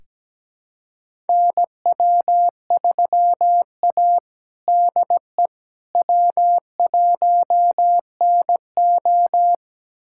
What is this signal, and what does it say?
Morse code, heard as a keyed tone, about 17 wpm, 700 Hz. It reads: NW3A DE W1NO
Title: USA Callsign DE Callsign - 4 Characters - 17wpm